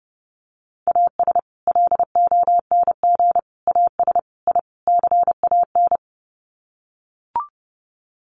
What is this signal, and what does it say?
Morse code, heard as a keyed tone, about 30 wpm, 700 Hz. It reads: AS LONG AS I CAN
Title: Sets of 5 Words Encore2 30wpm